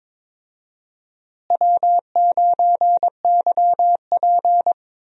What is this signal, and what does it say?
Morse code, heard as a keyed tone, about 21 wpm, 700 Hz. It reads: W9YP
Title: USA Call Signs - 4 Characters - Repeated 3x - 22wpm